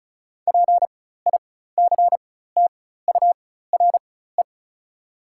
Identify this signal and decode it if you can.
Morse code, heard as a keyed tone, about 36 wpm, spaced as for 16 wpm, 690 Hz. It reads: PICTURE
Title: 4x Character Spacing - Top 500 Words - 35wpm